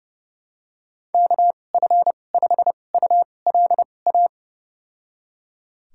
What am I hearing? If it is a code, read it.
Morse code, heard as a keyed tone, about 30 wpm, spaced as for 22 wpm, 700 Hz. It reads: KF5ULA
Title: USA Call Signs - 6 Characters - 2x Character Spacing - 30wpm